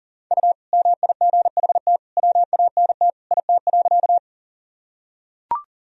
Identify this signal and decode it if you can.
Morse code, heard as a keyed tone, about 40 wpm, 700 Hz. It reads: U MIGHT WANT IT.
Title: Sentences from Top 300 Words and Abbreviations 40wpm